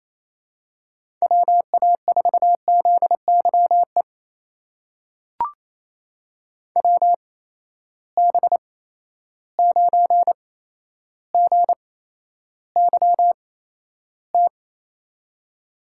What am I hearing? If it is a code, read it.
Morse code, heard as a keyed tone, about 28 wpm, spaced as for 12 wpm, 700 Hz. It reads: WA4ZYE WB9GYT
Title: USA Call Signs - 6 Characters - 8x Character Spacing - 28wpm